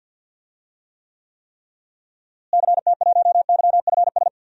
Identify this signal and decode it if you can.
Morse code, heard as a keyed tone, about 52 wpm, 700 Hz. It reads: CT1YFS
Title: DXCC-100 Call Signs - 6 Characters - Repeated 3x - 50wpm